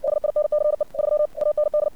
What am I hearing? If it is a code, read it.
Morse code, heard as a keyed tone, about 57 wpm, 600 Hz. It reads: SINCE 3 AND
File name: archived / CRSnet